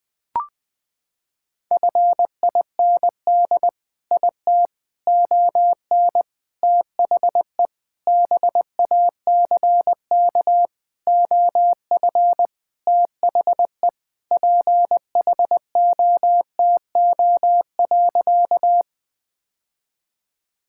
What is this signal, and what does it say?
Morse code, heard as a keyed tone, about 20 wpm, 700 Hz. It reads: FIND IT ON THE BACK OF THE PHOTO.
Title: Sentences with 4-Word Phrases - 20wpm